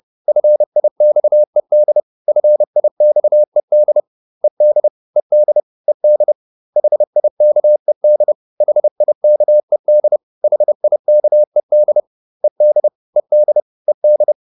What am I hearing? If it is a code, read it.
Morse code, heard as a keyed tone, about 30 wpm, 600 Hz. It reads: FIXED FIXED ED ED ED HIKED HIKED HIKED ED ED ED